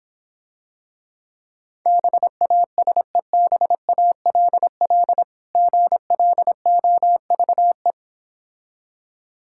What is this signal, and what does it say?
Morse code, heard as a keyed tone, about 26 wpm, 700 Hz. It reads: BASEBALL GLOVE